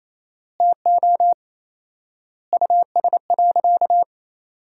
Morse code, heard as a keyed tone, about 28 wpm, 700 Hz. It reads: TO US.
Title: 4x Word-Spacing - Sentences from Top 200 Words - 28wpm